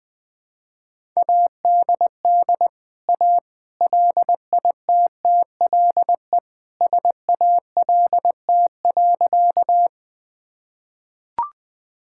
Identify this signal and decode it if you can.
Morse code, heard as a keyed tone, about 21 wpm, 700 Hz. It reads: ADD A LITTLE SALT.
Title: Sentences from Top 1000 Words - Review Part 5 - 20wpm